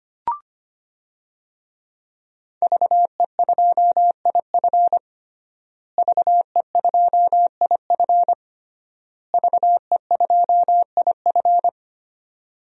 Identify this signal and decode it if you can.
Morse code, heard as a keyed tone, about 25 wpm, 700 Hz. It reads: VE2IF VE2IF VE2IF
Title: DXCC-100 Call Signs - 5 Characters - Repeated 3x - 25wpm